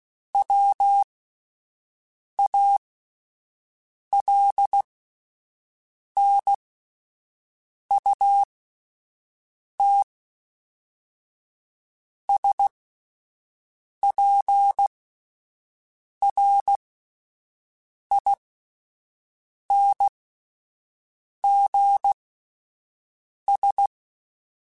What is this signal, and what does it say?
Morse code, heard as a keyed tone, about 16 wpm, spaced as for 5 wpm, 800 Hz. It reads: WALNUT SPRINGS